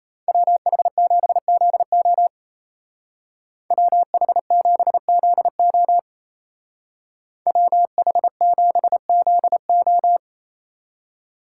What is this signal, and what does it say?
Morse code, heard as a keyed tone, about 33 wpm, 700 Hz. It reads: WH7ZO WH7ZO WH7ZO
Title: Speed Racing - USA 2x2 Call Signs 28wpm